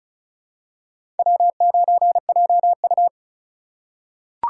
Morse code, heard as a keyed tone, about 34 wpm, 700 Hz. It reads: W9JU
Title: USA Call Signs - 4 Characters - 2x Character Spacing - 35wpm